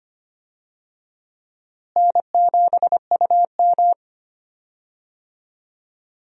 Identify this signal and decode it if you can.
Morse code, heard as a keyed tone, about 25 wpm, 700 Hz. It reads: N7UM
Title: USA Call Signs - 4 Characters - Repeated 3x - 25wpm